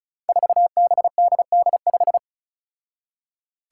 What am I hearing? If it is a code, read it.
Morse code, heard as a keyed tone, about 34 wpm, 700 Hz. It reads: 4BDD5
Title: Five Challenging Letter-Number Combinations 35wpm